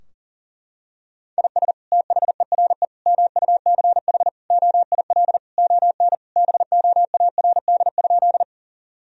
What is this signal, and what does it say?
Morse code, heard as a keyed tone, about 40 wpm, 700 Hz. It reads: IS THERE MUCH OIL ON BOARD?
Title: Sentences from Top 1000 Words - Review Part 4 - 40wpm